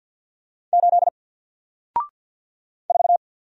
Morse code, heard as a keyed tone, about 50 wpm, 700 Hz. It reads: 8 4
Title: Single Letter-Number 50wpm